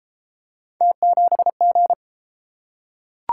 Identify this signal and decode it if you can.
Morse code, heard as a keyed tone, about 27 wpm, 700 Hz. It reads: T7G
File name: Speed-Racing - Three Letter-Number Combinations - 22wpm